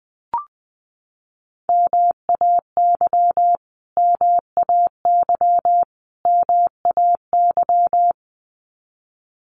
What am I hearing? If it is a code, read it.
Morse code, heard as a keyed tone, about 20 wpm, 700 Hz. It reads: MAY MAY MAY
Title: Top 500 Words Repeated 20wpm